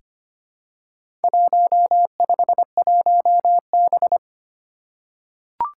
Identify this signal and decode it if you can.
Morse code, heard as a keyed tone, about 25 wpm, 700 Hz. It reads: 151B
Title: Four Challenging Letter-Number Combinations 25wpm